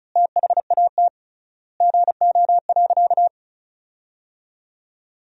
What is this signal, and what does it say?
Morse code, heard as a keyed tone, about 36 wpm, 700 Hz. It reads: THAT GO.
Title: 3x Word-Spacing - Sentences from Top 300 Words - 35wpm